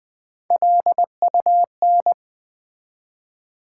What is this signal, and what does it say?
Morse code, heard as a keyed tone, about 20 wpm, 700 Hz. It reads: LUN
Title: Build A Word - 1 Letter At A Time - 5-Letter Words - 20wpm